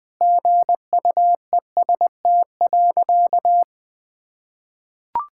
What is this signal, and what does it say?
Morse code, heard as a keyed tone, about 21 wpm, 700 Hz. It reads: GUEST.
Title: Sentences from Top 1600 Words 20wpm